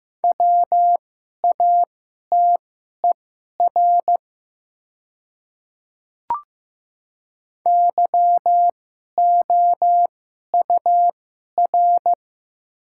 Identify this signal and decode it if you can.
Morse code, heard as a keyed tone, about 15 wpm, spaced as for 11 wpm, 700 Hz. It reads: WATER YOUR
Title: 2x Character Spacing - Top 300 Words - 15wpm